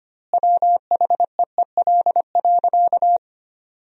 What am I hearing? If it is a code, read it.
Morse code, heard as a keyed tone, about 25 wpm, 700 Hz. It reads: WHEEL.
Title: 2x Word-Spacing - Sentences from Top 500 Words - 25wpm